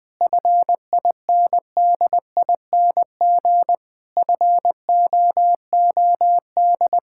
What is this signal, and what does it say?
Morse code, heard as a keyed tone, about 20 wpm, 700 Hz. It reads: FINDING FOOD
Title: Sentences from Top 1600 Words 20wpm